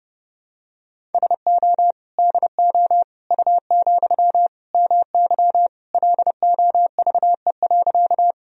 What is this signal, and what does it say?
Morse code, heard as a keyed tone, about 30 wpm, 700 Hz. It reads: SO DO U, MY LOVE.
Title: Sentences from Top 400 Words and Abbreviations 30wpm